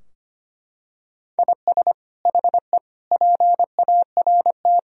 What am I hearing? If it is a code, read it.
Morse code, heard as a keyed tone, about 25 wpm, 700 Hz. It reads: IS HE PART
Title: Sentences from Top 1000 Words - Review Part 2 - 25wpm